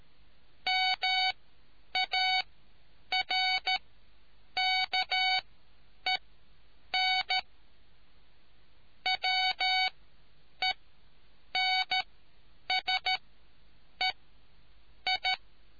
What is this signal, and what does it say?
Morse code, heard as a keyed tone, about 13 wpm, spaced as for 8 wpm, 750 Hz. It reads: MARKEN WENSE I